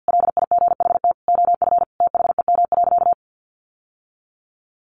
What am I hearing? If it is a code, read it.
Morse code, heard as a keyed tone, about 50 wpm, 700 Hz. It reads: LIGHT OF THEM.
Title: Sentences from Top 1300 Words 50wpm